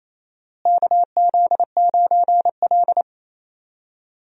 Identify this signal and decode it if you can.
Morse code, heard as a keyed tone, about 27 wpm, 700 Hz. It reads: KZ9L
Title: USA Call Signs - 4 Characters - Repeated 2x - 28wpm